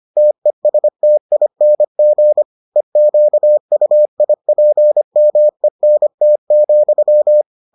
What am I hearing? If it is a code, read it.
Morse code, heard as a keyed tone, about 25 wpm, 600 Hz. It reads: TESTING EQUIPMENT,